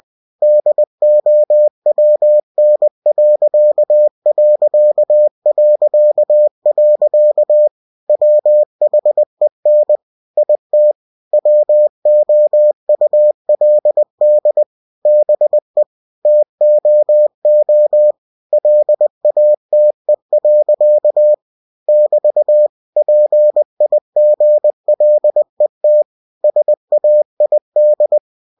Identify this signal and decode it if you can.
Morse code, heard as a keyed tone, about 20 wpm, 600 Hz. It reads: DOWN.... WHEN IT WOULD BE TOO LATE. <BT> PIGLET SAID
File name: cw-books / winnie-the-pooh / winnie-20wpm-page-0035